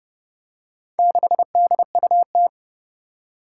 Morse code, heard as a keyed tone, about 30 wpm, 700 Hz. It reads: 6DUT
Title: Four Challenging Letter-Number Combinations 30wpm